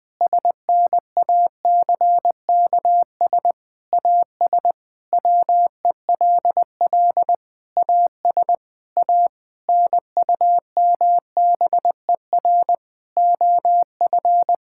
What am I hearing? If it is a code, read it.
Morse code, heard as a keyed tone, about 20 wpm, 700 Hz. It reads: SNACKS AS WELL AS A NUMBER OF
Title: Sentences with 6-Word Phrases - Spoken Beginning - 20wpm